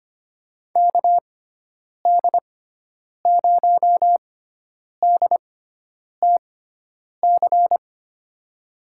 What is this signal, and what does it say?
Morse code, heard as a keyed tone, about 25 wpm, spaced as for 9 wpm, 700 Hz. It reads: KD0DTC
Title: USA Call Signs - 6 Characters - 6x Character Spacing - 25wpm